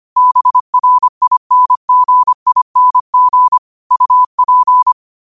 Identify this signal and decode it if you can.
Morse code, heard as a keyed tone, about 25 wpm, 1000 Hz. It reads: BRINGING UP
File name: qotdfiles / 25wpm